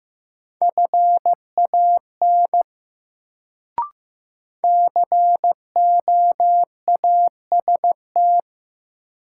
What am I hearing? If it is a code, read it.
Morse code, heard as a keyed tone, about 15 wpm, 700 Hz. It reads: FAN COAST
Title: Sentences from Top 1300 Words - New Word Drill Alt - 15wpm